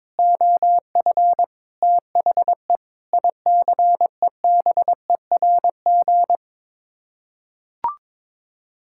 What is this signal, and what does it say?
Morse code, heard as a keyed tone, about 21 wpm, 700 Hz. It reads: OF THE ICEBERG
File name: practice / 22wpm / Sets of 8 Words - 22wpm